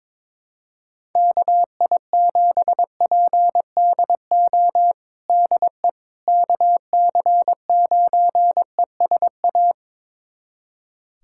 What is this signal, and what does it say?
Morse code, heard as a keyed tone, about 21 wpm, 700 Hz. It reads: KI7PDO DE KC9ESA
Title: USA Callsign DE Callsign - 6 Characters - 22wpm